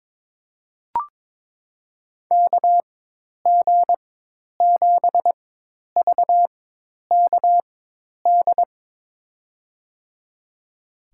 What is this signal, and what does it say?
Morse code, heard as a keyed tone, about 22 wpm, spaced as for 10 wpm, 700 Hz. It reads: KG7VKD
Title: USA Call Signs - 6 Characters - 4x Character Spacing - 22wpm